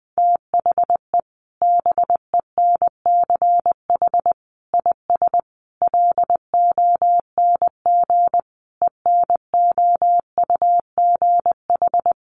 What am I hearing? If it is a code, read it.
Morse code, heard as a keyed tone, about 20 wpm, 700 Hz. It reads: THE BENCH IS LONG ENOUGH
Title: Sentences from Top 1400 Words 20wpm